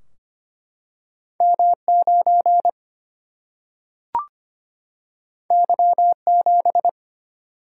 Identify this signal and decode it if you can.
Morse code, heard as a keyed tone, about 25 wpm, 700 Hz. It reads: M9 Y7
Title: Single-Letter Single-Number 25wpm